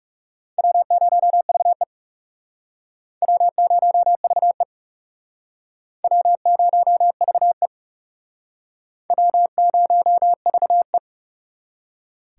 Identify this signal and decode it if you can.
Morse code, heard as a keyed tone, about 38 wpm, 700 Hz. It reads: W0VE W0VE W0VE W0VE